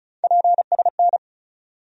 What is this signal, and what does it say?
Morse code, heard as a keyed tone, about 36 wpm, 700 Hz. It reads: PSN
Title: USA Call Sign Postfixes - 3 Letters 35wpm